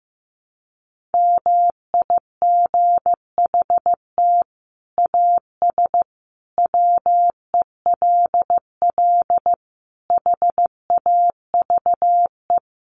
Morse code, heard as a keyed tone, about 14 wpm, 700 Hz. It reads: MIGHT AS WELL HAVE
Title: Sets of 5 Words Encore2 15wpm